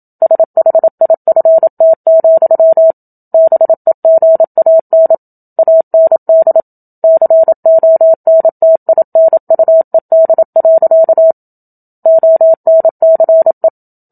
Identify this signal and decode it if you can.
Morse code, heard as a keyed tone, about 28 wpm, 650 Hz. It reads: SHIFT, BEGAN AND CONTINUED. ONCE
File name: Book Two / 01-Chapter 1 - 27 WPM